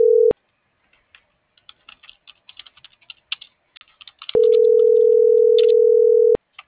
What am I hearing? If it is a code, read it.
Ringback tone (North America), 440 + 480 Hz.